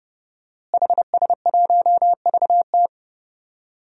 Morse code, heard as a keyed tone, about 30 wpm, 700 Hz. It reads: HS1VT
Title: DXCC-100 Call Signs - 5 Characters - Repeated 3x - 30wpm